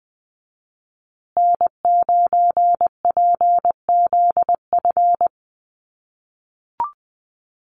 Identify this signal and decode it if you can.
Morse code, heard as a keyed tone, about 20 wpm, 700 Hz. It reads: N9PZF
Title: USA Call Signs Repeated 3x 20wpm